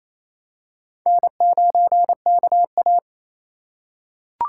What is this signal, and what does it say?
Morse code, heard as a keyed tone, about 28 wpm, 700 Hz. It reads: N9KA
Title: USA Call Signs - 4 Characters - 3x Character Spacing - 28wpm